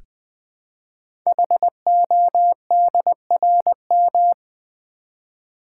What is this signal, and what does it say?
Morse code, heard as a keyed tone, about 20 wpm, 700 Hz. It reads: HODRM
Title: Code Groups - 5 Letters - 20wpm